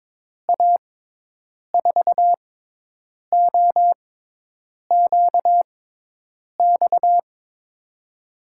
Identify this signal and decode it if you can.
Morse code, heard as a keyed tone, about 22 wpm, spaced as for 8 wpm, 700 Hz. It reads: A4OQX
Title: USA Call Signs - 6 Characters - 6x Character Spacing - 22wpm